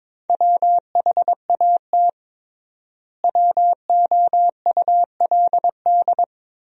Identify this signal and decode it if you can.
Morse code, heard as a keyed tone, about 22 wpm, 700 Hz. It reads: WHAT WOULD
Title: 3x Word-Spacing - Sentences from Top 300 Words - 22wpm